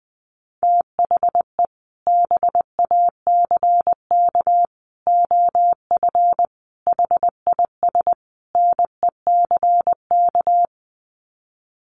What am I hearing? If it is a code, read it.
Morse code, heard as a keyed tone, about 20 wpm, 700 Hz. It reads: THE BACK OF HIS NECK
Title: Sets of 5 Words Encore 20wpm